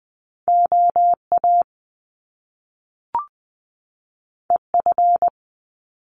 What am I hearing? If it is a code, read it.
Morse code, heard as a keyed tone, about 20 wpm, 700 Hz. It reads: O A E F
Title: Top Two Letter Combinations in Words 20wpm